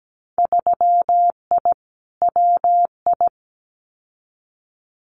Morse code, heard as a keyed tone, about 17 wpm, 700 Hz. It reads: 3I WI
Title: Winter Field Day Exchanges 17wpm